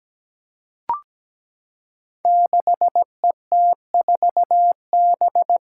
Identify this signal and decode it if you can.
Morse code, heard as a keyed tone, about 17 wpm, 700 Hz. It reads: 6ET4B
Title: Five Challenging Letter-Number Combinations 17wpm